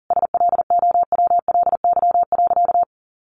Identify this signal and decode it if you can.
Morse code, heard as a keyed tone, about 40 wpm, 700 Hz. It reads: SLOWLY.
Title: Sentences from Top 1100 Words 40wpm